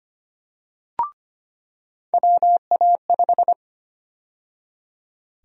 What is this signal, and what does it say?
Morse code, heard as a keyed tone, about 25 wpm, 700 Hz. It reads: WA5
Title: USA Call Sign Prefixes - 25wpm